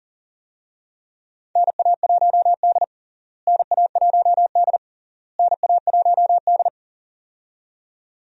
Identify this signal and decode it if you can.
Morse code, heard as a keyed tone, about 40 wpm, 700 Hz. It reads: NA1D NA1D NA1D
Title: USA Call Signs - 4 Characters - Repeated 3x - 40wpm